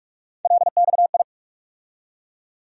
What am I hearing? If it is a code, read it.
Morse code, heard as a keyed tone, about 43 wpm, 700 Hz. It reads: LXI
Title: USA Call Sign Postfixes - 3 Letters 45wpm